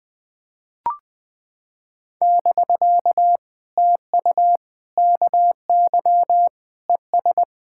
Morse code, heard as a keyed tone, about 20 wpm, 700 Hz. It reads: <BK> TU KY ES
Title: POTA QSO Practice - 20wpm